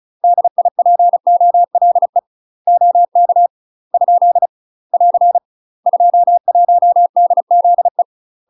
Morse code, heard as a keyed tone, about 34 wpm, 700 Hz. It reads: DIPOLE OK ? <AR> 21DZE
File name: UK MASTER QSOs TRAINING / 35WPM / UK MASTER QSOs 0002